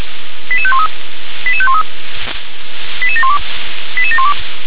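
Five-tone selective call (ZVEI), heard as quick runs of five tones, tones of about 70 ms. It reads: 88513 88513 88712 88712
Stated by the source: ZVEI five-tone call 88720 received by SDR.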